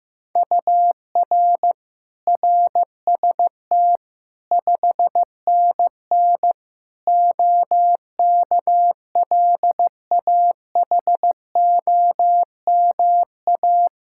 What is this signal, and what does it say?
Morse code, heard as a keyed tone, about 16 wpm, 700 Hz. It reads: UR RST 5NN OKLAHOMA
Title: Practice QSO - Exchange 1 - Beginning - 15wpm